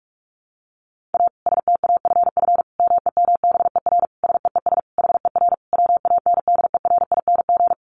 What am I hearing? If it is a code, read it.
Morse code, heard as a keyed tone, about 44 wpm, 700 Hz. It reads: A STAFF MEMBER SEES HER WANDERING
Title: Additional Easy Phrases 45wpm